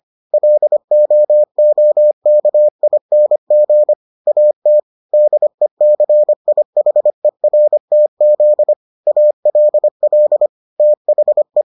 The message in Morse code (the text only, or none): LOOKING AT DECIHERTZ ALL THE